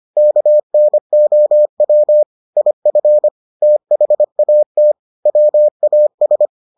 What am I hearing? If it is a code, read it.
Morse code, heard as a keyed tone, about 25 wpm, 600 Hz. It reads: KNOW IF THAT WAS